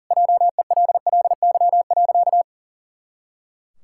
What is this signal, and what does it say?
Morse code, heard as a keyed tone, about 40 wpm, 700 Hz. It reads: JELLY.
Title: Pangram Challenge v2 - 40wpm